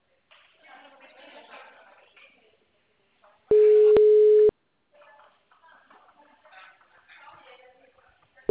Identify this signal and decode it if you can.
Ringback tone (Europe, 425 Hz), with speech.